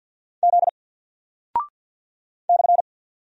Morse code, heard as a keyed tone, about 48 wpm, 700 Hz. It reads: Z /
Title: Single Character (Letter:Number:Punctuation:Prosign) Alt 50wpm